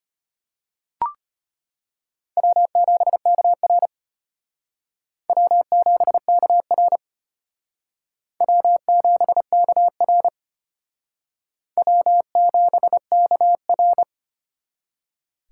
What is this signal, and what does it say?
Morse code, heard as a keyed tone, about 32 wpm, 700 Hz. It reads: W7KR W7KR W7KR W7KR